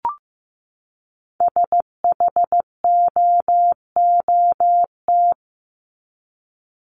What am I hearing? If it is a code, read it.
Morse code, heard as a keyed tone, about 15 wpm, 700 Hz. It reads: SHOOT
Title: Top 1001-1250 Words 15wpm